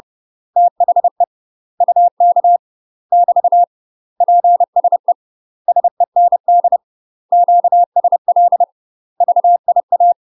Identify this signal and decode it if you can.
Morse code, heard as a keyed tone, about 30 wpm, 700 Hz. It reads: THE UK <BT> PSE SEND QSL VIA
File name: UK MASTER QSOs TRAINING / 30WPM / UK MASTER QSOs 0050